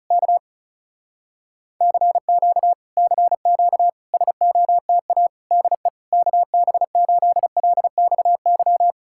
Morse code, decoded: K CQ CQ SOTA DE KB8LXY

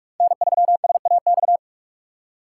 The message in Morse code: N3SAX

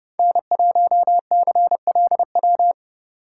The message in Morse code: N1CLW

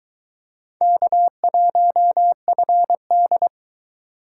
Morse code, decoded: K1FD

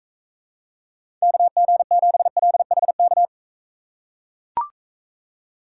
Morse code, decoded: KG7LHK

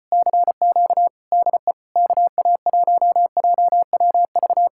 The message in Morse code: CQ DE KA1JWV

